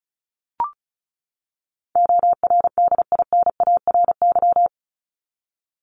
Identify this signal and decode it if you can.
Morse code, heard as a keyed tone, about 34 wpm, 700 Hz. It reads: ORDINARY
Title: Top 1751-2000 Words 35wpm